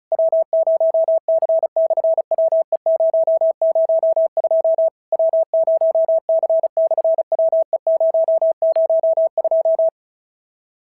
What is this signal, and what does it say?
Morse code, heard as a keyed tone, about 34 wpm, 650 Hz. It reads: W0C/WE002 W0C/WE002
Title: SOTA QSO Practice 35wpm